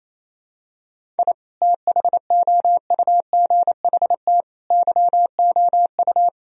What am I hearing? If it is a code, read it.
Morse code, heard as a keyed tone, about 28 wpm, 700 Hz. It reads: I THOUGHT YOU
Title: Sentences from Top 1000 Words - Review Part 1 - 28wpm